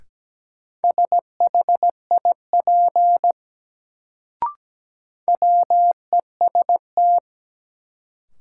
Morse code, heard as a keyed tone, about 17 wpm, 700 Hz. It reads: SHIP WEST
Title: Sentences from Top 400 Words - New Word Drill - 17wpm